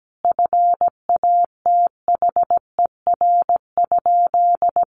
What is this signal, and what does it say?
Morse code, heard as a keyed tone, about 17 wpm, 700 Hz. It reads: FATHER?
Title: Sentences from Top 300 Words 17wpm